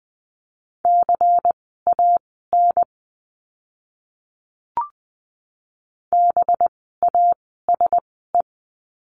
Morse code, heard as a keyed tone, about 20 wpm, spaced as for 15 wpm, 700 Hz. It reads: CAN BASE
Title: 185 Lesson 30f - Letter b - Words 20wpm